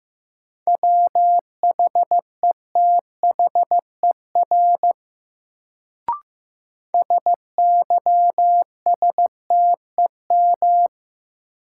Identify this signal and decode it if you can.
Morse code, heard as a keyed tone, about 15 wpm, 700 Hz. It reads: WHETHER SYSTEM
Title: Sentences from Top 1000 Words - Review Part 3 - New Word Drill - 15wpm